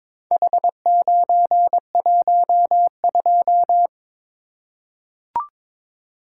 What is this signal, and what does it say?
Morse code, heard as a keyed tone, about 23 wpm, 700 Hz. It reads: H912
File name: practice / 22wpm / Four Letter-Number Combinations - 22wpm